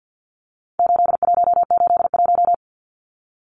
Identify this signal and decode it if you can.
Morse code, heard as a keyed tone, about 50 wpm, 700 Hz. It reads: 8181
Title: Numbers - 4-Digit 50wpm